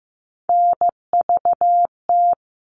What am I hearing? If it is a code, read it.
Morse code, heard as a keyed tone, about 15 wpm, 700 Hz. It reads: NVT
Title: All Cut Numbers - 3-Digit 15wpm